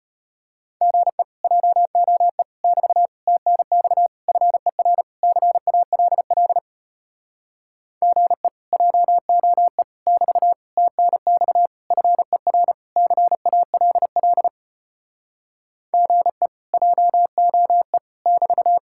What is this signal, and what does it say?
Morse code, heard as a keyed tone, about 33 wpm, 700 Hz. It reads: GE JOE <BT> TNX FER CALL GE JOE <BT> TNX FER CALL GE JOE <BT>